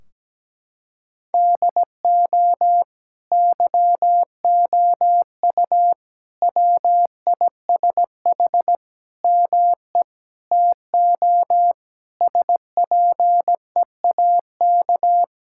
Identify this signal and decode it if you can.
Morse code, heard as a keyed tone, about 17 wpm, 700 Hz. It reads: DO YOU WISH ME TO SPEAK
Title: Sentences from Top 1600 Words 17wpm